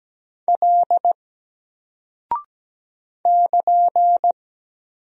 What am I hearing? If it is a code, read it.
Morse code, heard as a keyed tone, about 17 wpm, 700 Hz. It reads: L <KN>
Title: Single Character (Letter:Number:Punctuation:Prosign) Alt 17wpm